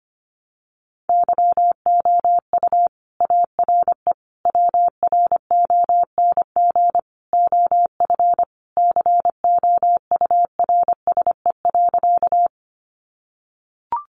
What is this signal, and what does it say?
Morse code, heard as a keyed tone, about 25 wpm, 700 Hz. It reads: YOU ARE WRONG OF COURSE.
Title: Sentences from Top 900 Words 25wpm